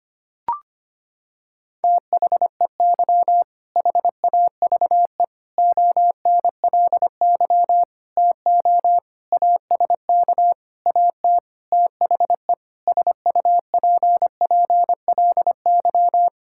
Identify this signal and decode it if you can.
Morse code, heard as a keyed tone, about 25 wpm, 700 Hz. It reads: THEY HAVE ONLY TO ASK AT THE SUPPLY
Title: Sentences from Top 1000 Words - Review Part 5 - 25wpm